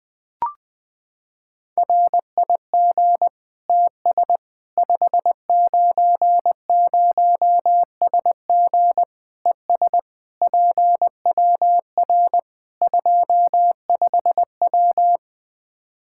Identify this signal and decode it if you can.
Morse code, heard as a keyed tone, about 20 wpm, 700 Hz. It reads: RIG TS 590SG ES PWR 25W
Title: K1USN SST QSO Practice 20wpm